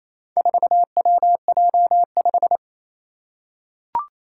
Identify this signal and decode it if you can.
Morse code, heard as a keyed tone, about 28 wpm, 700 Hz. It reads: 4WJ5
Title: Four Challenging Letter-Number Combinations 28wpm